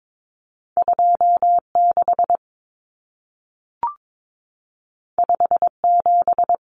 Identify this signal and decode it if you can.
Morse code, heard as a keyed tone, about 22 wpm, 700 Hz. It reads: 26 57
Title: Common Cut Numbers - 2-Digit 22wpm